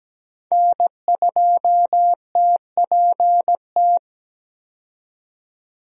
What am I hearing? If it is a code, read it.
Morse code, heard as a keyed tone, about 17 wpm, 700 Hz. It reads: N2TPT
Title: USA Call Signs - 5 Characters - 17wpm